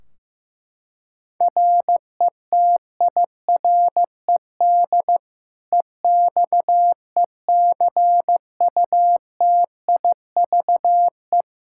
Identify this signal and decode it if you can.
Morse code, heard as a keyed tone, about 15 wpm, 700 Hz. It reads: RETIRED EXECUTIVE
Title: Practice QSO - Exchange 3 - More Info - 15wpm